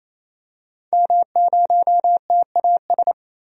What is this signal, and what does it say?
Morse code, heard as a keyed tone, about 28 wpm, 700 Hz. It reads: M0TAS
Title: DXCC-100 Call Signs - 5 Characters - 28wpm